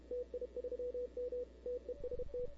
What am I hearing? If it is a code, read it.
Morse code, heard as a keyed tone, about 32 wpm, 500 Hz. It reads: TI3M TEST
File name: CW / 2005-k5tr arrl cw dx -00:00